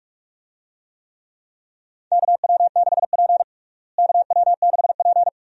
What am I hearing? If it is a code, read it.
Morse code, heard as a keyed tone, about 46 wpm, 700 Hz. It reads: KW6P KW6P
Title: USA Call Signs - 4 Characters - Repeated 2x - 45wpm